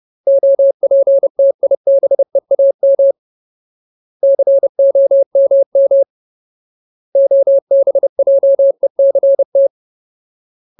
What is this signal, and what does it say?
Morse code, heard as a keyed tone, about 30 wpm, 550 Hz. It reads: OPTIBEAM COMM OBJECT
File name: Ultimate Word List / 30WPM Ultimate